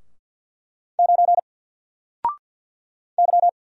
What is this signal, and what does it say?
Morse code, heard as a keyed tone, about 50 wpm, 700 Hz. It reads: 9 Y
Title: Single Letter-Number Alt 50wpm